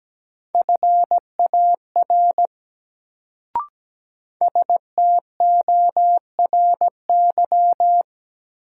Morse code, heard as a keyed tone, about 17 wpm, 700 Hz. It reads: FAR STORY
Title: Sentences from Top 300 Words - New Word Drill - 17wpm